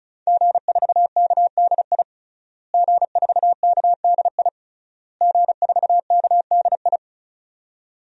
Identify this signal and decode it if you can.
Morse code, heard as a keyed tone, about 34 wpm, 700 Hz. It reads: G4KDI G4KDI G4KDI
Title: DXCC-100 Call Signs - 5 Characters - Repeated 3x - 35wpm